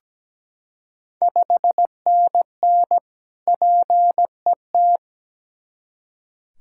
Morse code, heard as a keyed tone, about 17 wpm, 700 Hz. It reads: 5NN PET
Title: Ontario QSO Party Exchange 17wpm